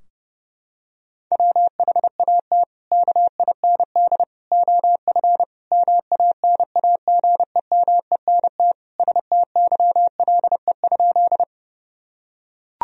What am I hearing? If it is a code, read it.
Morse code, heard as a keyed tone, about 30 wpm, 700 Hz. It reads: WHAT KIND OF MANAGEMENT STYLE?
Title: Sentences from Top 1000 Words - Review Part 4 - 30wpm